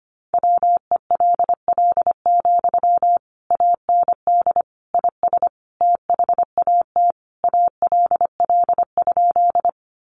Morse code, heard as a keyed tone, about 25 wpm, 700 Hz. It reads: WELL, AND IS THAT ALL?
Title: Sentences from Top 200 Words 25wpm